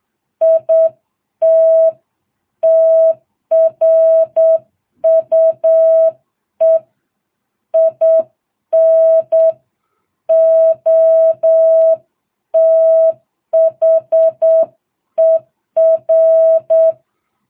Morse code, heard as a keyed tone, about 8 wpm, 660 Hz. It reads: IT TRUE IN OTHER